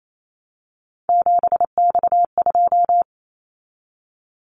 Morse code, heard as a keyed tone, about 28 wpm, 700 Hz. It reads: 7X2
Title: Callsign Prefixes for Top 100 DXCC - 28wpm